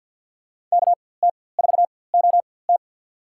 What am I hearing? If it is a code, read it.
Morse code, heard as a keyed tone, about 53 wpm, spaced as for 24 wpm, 690 Hz. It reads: KT4OT
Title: USA Call Signs - 5 Characters - 4x Character Spacing - 50wpm